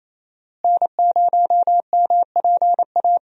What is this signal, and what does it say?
Morse code, heard as a keyed tone, about 28 wpm, 700 Hz. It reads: N0MPA
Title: USA Call Signs - 5 Characters - Repeated 2x - 28wpm